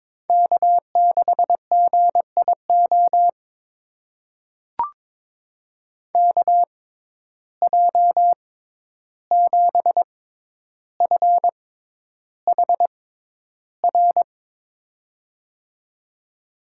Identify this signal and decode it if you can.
Morse code, heard as a keyed tone, about 23 wpm, spaced as for 11 wpm, 700 Hz. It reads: K6GIO KJ7FHR
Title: USA Call Signs - 6x Character Spacing - 22wpm